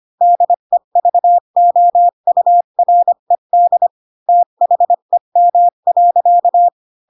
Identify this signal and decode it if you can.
Morse code, heard as a keyed tone, about 25 wpm, 700 Hz. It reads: DEVOURED THEM.
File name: Pinocchio / Pinocchio ch7 20 25 ik1hge